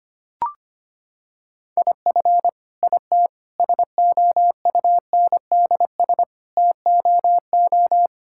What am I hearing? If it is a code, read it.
Morse code, heard as a keyed tone, about 25 wpm, 700 Hz. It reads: IF IT SOUNDS TOO
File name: Sets of 9 Words - 25wpm